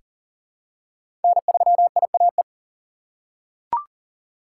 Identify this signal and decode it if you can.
Morse code, heard as a keyed tone, about 40 wpm, 700 Hz. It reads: N3IAE
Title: USA Call Signs - 5 Characters - 3x Character Spacing - 40wpm